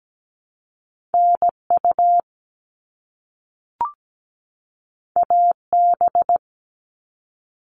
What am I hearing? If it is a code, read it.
Morse code, heard as a keyed tone, about 17 wpm, 700 Hz. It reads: NU AB
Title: Top Two Letter Combinations in Words 17wpm